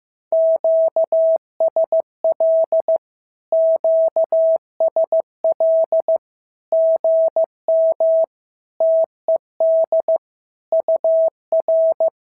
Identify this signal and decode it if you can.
Morse code, heard as a keyed tone, about 15 wpm, 650 Hz. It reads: QSL QSL GM TED UR
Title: SOTA QSO Practice 15wpm